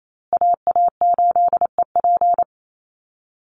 Morse code, heard as a keyed tone, about 28 wpm, 700 Hz. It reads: AA8EP
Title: USA Call Signs Repeated 3x 28wpm